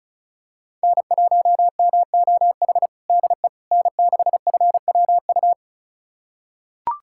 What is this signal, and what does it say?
Morse code, heard as a keyed tone, about 36 wpm, 700 Hz. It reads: N1MOH DE N6FWU
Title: USA Callsign DE Callsign - 5 Characters - 35wpm